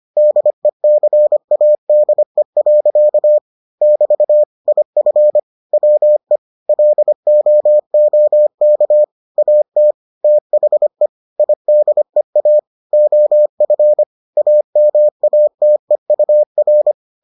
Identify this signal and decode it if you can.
Morse code, heard as a keyed tone, about 25 wpm, 600 Hz. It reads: DECADE. <BT> IF WE LOOK AT THE IDEA OF AMATEUR